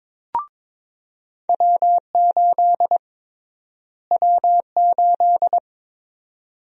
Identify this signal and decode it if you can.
Morse code, heard as a keyed tone, about 22 wpm, 700 Hz. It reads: W8 W8
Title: USA Call Sign Prefixes - Repeated 2x - 22wpm